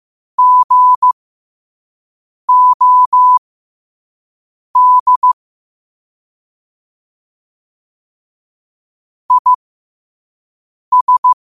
Morse code, heard as a keyed tone, about 15 wpm, spaced as for 5 wpm, 1000 Hz. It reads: GOD IS
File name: qotdfiles / 5wpm